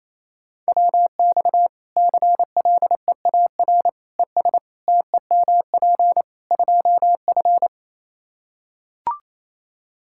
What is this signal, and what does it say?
Morse code, heard as a keyed tone, about 28 wpm, 700 Hz. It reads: WX CLEAR ES TEMP 2F
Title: Practice QSO - Exchange 2 - Info - 28wpm